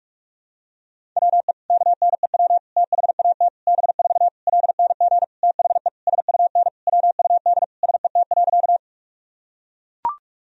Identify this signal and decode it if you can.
Morse code, heard as a keyed tone, about 42 wpm, 700 Hz. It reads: WE KNEW THAT B4 LNG THE SUN WUD SET.
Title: Sentences from Top 400 Words and Abbreviations 45wpm